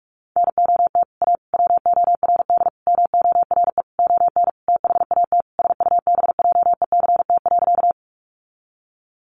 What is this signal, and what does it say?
Morse code, heard as a keyed tone, about 44 wpm, 700 Hz. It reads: NOT A WORD MORE ON THAT SUBJECT.
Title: Sentences from Top 800 Words 45wpm